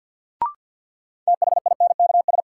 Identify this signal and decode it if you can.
Morse code, heard as a keyed tone, about 48 wpm, 700 Hz. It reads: THINKS
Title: Sentences from Top 1000 Words - Review Part 6 - New Word Drill - 50wpm